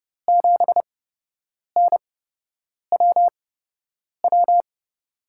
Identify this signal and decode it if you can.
Morse code, heard as a keyed tone, about 30 wpm, spaced as for 8 wpm, 700 Hz. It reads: 7NWW
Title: USA Call Signs - 5 Characters - 8x Character Spacing - 30wpm